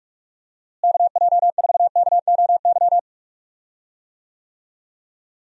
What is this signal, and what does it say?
Morse code, heard as a keyed tone, about 45 wpm, 700 Hz. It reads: KJ4KOY
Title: USA Call Signs - 6 Characters - Repeated 3x - 45wpm